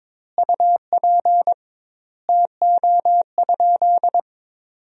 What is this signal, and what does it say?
Morse code, heard as a keyed tone, about 22 wpm, 700 Hz. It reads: UP TO?
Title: 2x Word-Spacing - Sentences from Top 100 Words - 22wpm